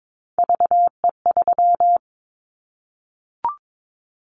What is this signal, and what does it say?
Morse code, heard as a keyed tone, about 22 wpm, 700 Hz. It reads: VE3
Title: Callsign Prefixes for Top 25 DXCC - Repeated 3x - 22wpm